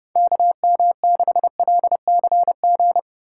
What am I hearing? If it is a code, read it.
Morse code, heard as a keyed tone, about 32 wpm, 700 Hz. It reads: KM6LCG